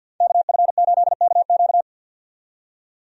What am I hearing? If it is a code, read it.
Morse code, heard as a keyed tone, about 49 wpm, 700 Hz. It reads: KF8KQ
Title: USA Call Signs - 5 Characters - 50wpm